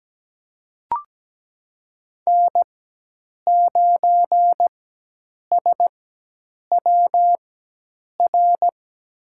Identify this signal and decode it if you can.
Morse code, heard as a keyed tone, about 17 wpm, spaced as for 8 wpm, 700 Hz. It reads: N9SWR
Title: USA Call Signs - 4x Character Spacing - 17wpm